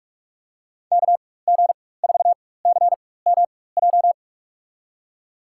Morse code, heard as a keyed tone, about 45 wpm, spaced as for 21 wpm, 700 Hz. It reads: KG4CMJ
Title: USA Call Signs - 4x Character Spacing - 45wpm